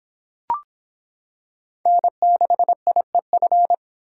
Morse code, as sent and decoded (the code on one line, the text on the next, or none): -. -.... .. ..-.
N6IEF